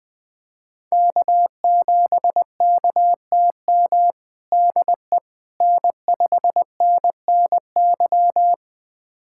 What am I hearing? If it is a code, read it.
Morse code, heard as a keyed tone, about 21 wpm, 700 Hz. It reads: K7KTM DE N5NNY
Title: USA Callsign DE Callsign 20wpm